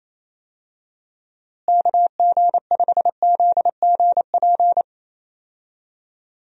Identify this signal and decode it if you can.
Morse code, heard as a keyed tone, about 27 wpm, 700 Hz. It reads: KG5ZGP
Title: USA Call Signs - 6 Characters - Repeated 3x - 28wpm